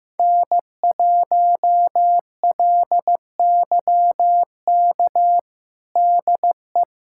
Morse code, heard as a keyed tone, about 15 wpm, 700 Hz. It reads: N1LYK DE
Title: USA Callsign DE Callsign - 5 Characters - 15wpm